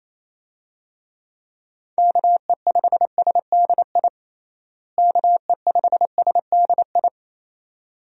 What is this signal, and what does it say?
Morse code, heard as a keyed tone, about 28 wpm, 700 Hz. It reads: KE5SDI KE5SDI
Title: USA Call Signs - 6 Characters - Repeated 2x - 28wpm